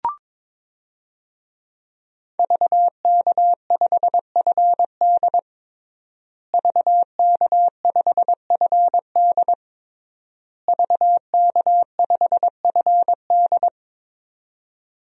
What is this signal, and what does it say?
Morse code, heard as a keyed tone, about 22 wpm, 700 Hz. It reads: VK5FD VK5FD VK5FD
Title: DXCC-100 Call Signs - 5 Characters - Repeated 3x - 22wpm